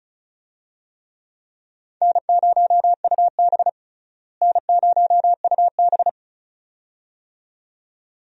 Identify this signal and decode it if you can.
Morse code, heard as a keyed tone, about 36 wpm, 700 Hz. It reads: N0UB N0UB
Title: USA Call Signs - 4 Characters - Repeated 2x - 35wpm